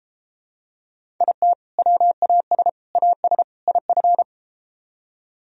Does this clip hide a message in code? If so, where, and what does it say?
Yes: Morse code from 1.2 to 4.2 s, heard as a keyed tone, about 33 wpm, 700 Hz. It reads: IT WAS AS IF